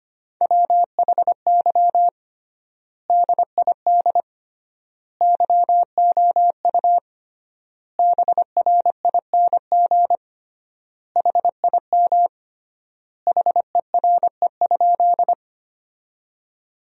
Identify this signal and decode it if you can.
Morse code, heard as a keyed tone, about 25 wpm, 700 Hz. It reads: WHY DID YOU BRING HIM HERE?
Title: 3x Word-Spacing - Sentences from Top 500 Words - 25wpm